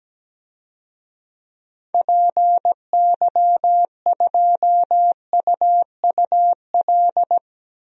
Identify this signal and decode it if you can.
Morse code, heard as a keyed tone, about 17 wpm, 700 Hz. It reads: PY2UUL
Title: DXCC-100 Call Signs - 6 Characters - 17wpm